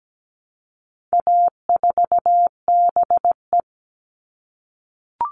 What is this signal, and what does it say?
Morse code, heard as a keyed tone, about 17 wpm, 700 Hz. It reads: A4BE
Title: All Cut Numbers - 4-Digit 17wpm